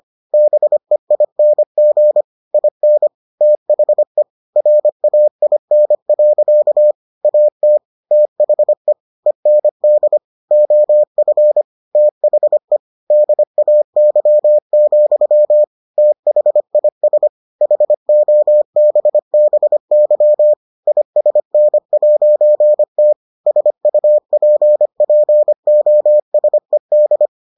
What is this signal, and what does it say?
Morse code, heard as a keyed tone, about 25 wpm, 600 Hz. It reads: BEING IN THE RAIN. AT THE END OF THE DAY, THIS HOBBY ISN'T SUPPOSED